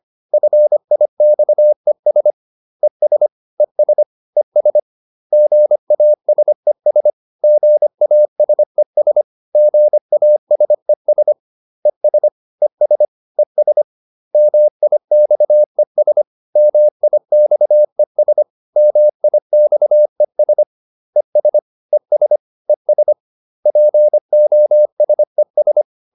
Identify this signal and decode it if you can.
Morse code, heard as a keyed tone, about 25 wpm, 600 Hz. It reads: FIXES ES ES ES GASES GASES GASES ES ES ES MIXES MIXES MIXES ES ES ES POSES